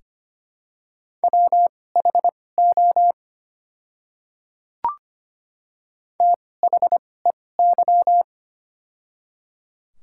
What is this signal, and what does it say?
Morse code, heard as a keyed tone, about 25 wpm, spaced as for 18 wpm, 700 Hz. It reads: WHO THEY
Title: 2x Character Spacing - Top 100 Words - 25wpm